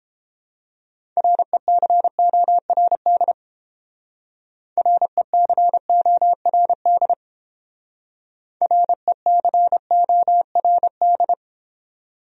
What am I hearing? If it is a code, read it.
Morse code, heard as a keyed tone, about 31 wpm, 700 Hz. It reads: RECORD RECORD RECORD